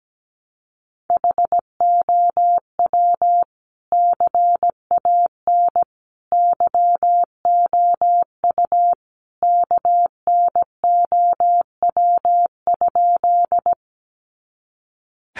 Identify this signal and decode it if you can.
Morse code, heard as a keyed tone, about 17 wpm, 700 Hz. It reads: HOW CAN YOU KNOW?
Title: Sentences from Top 100 Words 17wpm